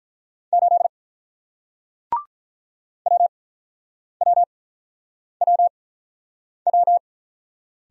Morse code, heard as a keyed tone, about 46 wpm, spaced as for 9 wpm, 700 Hz. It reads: 8 WWWW